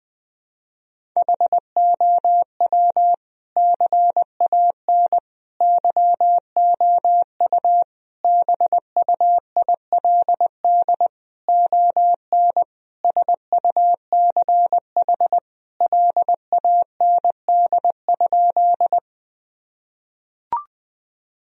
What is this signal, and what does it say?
Morse code, heard as a keyed tone, about 21 wpm, 700 Hz. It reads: HOW CAN YOU BUILD ON SUCH LAND?
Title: Sentences from Top 1000 Words - Review Part 1 - 20wpm